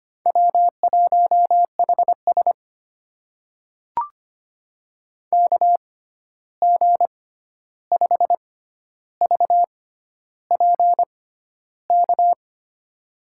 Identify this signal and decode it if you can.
Morse code, heard as a keyed tone, about 25 wpm, spaced as for 11 wpm, 700 Hz. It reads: W1HS KG5VPK
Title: USA Call Signs - 6x Character Spacing - 25wpm